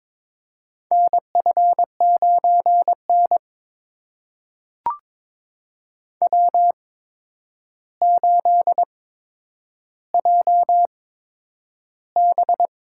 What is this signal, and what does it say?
Morse code, heard as a keyed tone, about 21 wpm, spaced as for 9 wpm, 700 Hz. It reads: NF9N W8JB
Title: USA Call Signs - 4 Characters - 8x Character Spacing - 22wpm